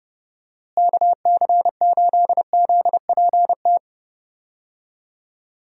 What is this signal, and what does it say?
Morse code, heard as a keyed tone, about 32 wpm, 700 Hz. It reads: KC8ZPT
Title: USA Call Signs - 6 Characters - Repeated 3x - 30wpm